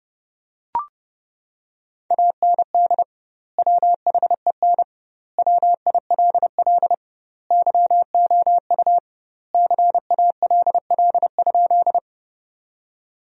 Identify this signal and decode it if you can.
Morse code, heard as a keyed tone, about 30 wpm, 700 Hz. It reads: AND WHEN WILL YOU CALL?
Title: 2x Word-Spacing - Sentences from Top 100 Words - 30wpm